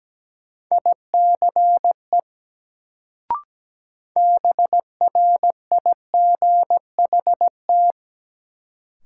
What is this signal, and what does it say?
Morse code, heard as a keyed tone, about 17 wpm, 700 Hz. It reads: ICE BRIGHT
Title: Sentences from Top 1000 Words - New Word Drill - 17wpm